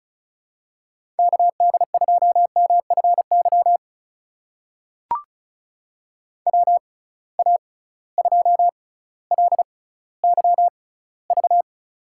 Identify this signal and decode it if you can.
Morse code, heard as a keyed tone, about 36 wpm, spaced as for 18 wpm, 700 Hz. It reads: KD2MFY WA2LYV
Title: USA Call Signs - 6 Characters - 6x Character Spacing - 35wpm